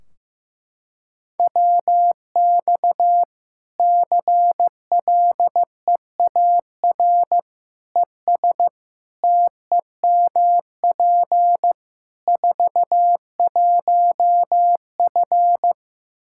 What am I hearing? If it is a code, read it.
Morse code, heard as a keyed tone, about 15 wpm, 700 Hz. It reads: WX CLEAR ES TEMP 41F
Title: K1USN SST QSO Practice 15wpm